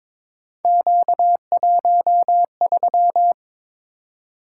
Morse code, heard as a keyed tone, about 22 wpm, 700 Hz. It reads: Q13
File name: practice / 22wpm / Three Letter-Number Combinations - 22wpm